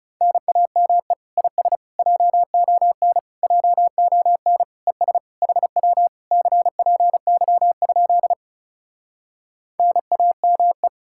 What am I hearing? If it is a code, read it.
Morse code, heard as a keyed tone, about 40 wpm, 700 Hz. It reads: NAME IS JON JON ES HW CPY? NAME